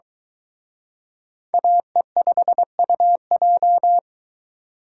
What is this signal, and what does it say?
Morse code, heard as a keyed tone, about 22 wpm, 700 Hz. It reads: AE5UJ